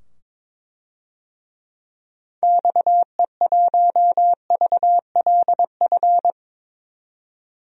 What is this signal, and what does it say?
Morse code, heard as a keyed tone, about 22 wpm, 700 Hz. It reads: XE1VLF
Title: DXCC-100 Call Signs - 6 Characters - 22wpm